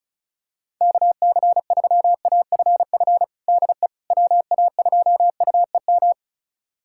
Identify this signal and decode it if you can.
Morse code, heard as a keyed tone, about 36 wpm, 700 Hz. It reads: KC3AFF DE WA2UEM
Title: USA Callsign DE Callsign - 6 Characters - 35wpm